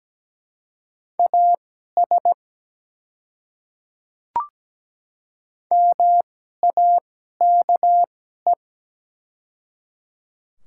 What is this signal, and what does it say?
Morse code, heard as a keyed tone, about 17 wpm, spaced as for 12 wpm, 700 Hz. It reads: AS MAKE
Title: 2x Character Spacing - Top 300 Words - 17wpm